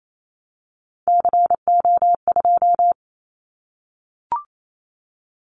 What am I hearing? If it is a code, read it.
Morse code, heard as a keyed tone, about 28 wpm, 700 Hz. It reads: CO2
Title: Callsign Prefixes for Top 100 DXCC - 28wpm